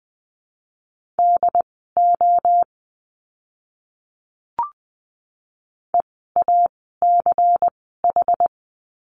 Morse code, heard as a keyed tone, about 20 wpm, spaced as for 14 wpm, 700 Hz. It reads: DO EACH
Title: 192 Lesson 31f - Letter v - Words 20wpm